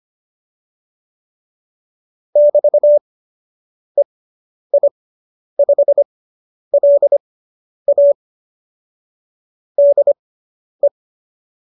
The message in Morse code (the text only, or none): <BT> EI5LA DE